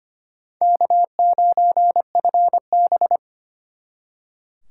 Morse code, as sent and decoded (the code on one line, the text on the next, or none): -.- ----. ..-. -...
K9FB